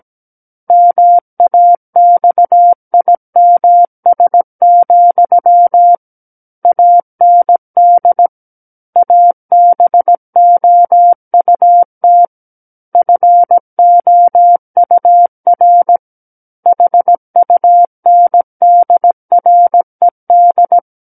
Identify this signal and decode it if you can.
Morse code, heard as a keyed tone, about 17 wpm, 700 Hz. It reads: MAXIMS, AND ABOUT FOUR HUNDRED
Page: SKCC Learning Center - Morse Code Practice Files